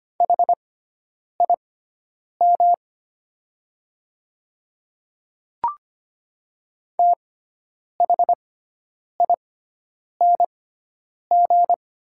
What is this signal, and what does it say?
Morse code, heard as a keyed tone, about 25 wpm, spaced as for 9 wpm, 700 Hz. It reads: HIM THING
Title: 6x Character Spacing - Top 200 Words - 25wpm